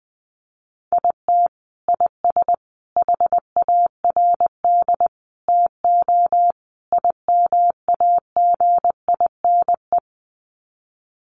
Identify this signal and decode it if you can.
Morse code, heard as a keyed tone, about 20 wpm, 700 Hz. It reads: IT IS HARD TO IMAGINE
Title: Sets of 5 Words Encore2 20wpm